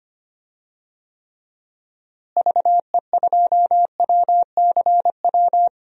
Morse code, heard as a keyed tone, about 25 wpm, 700 Hz. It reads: VE2WCW
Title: DXCC-100 Call Signs - 6 Characters - Repeated 3x - 25wpm